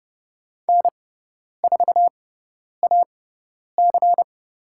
Morse code, heard as a keyed tone, about 30 wpm, spaced as for 10 wpm, 700 Hz. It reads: N4AC
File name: Lesson 18g - Letter m - Call Signs 30wpm